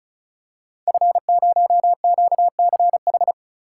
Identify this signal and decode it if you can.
Morse code, heard as a keyed tone, about 35 wpm, 700 Hz. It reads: F0QCH
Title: Code Groups - 5 Letter:Numbers - 35wpm